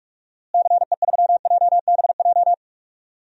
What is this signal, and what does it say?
Morse code, heard as a keyed tone, about 43 wpm, 700 Hz. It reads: CE3JBJ
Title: DXCC-100 Call Signs - 6 Characters - Repeated 2x - 45wpm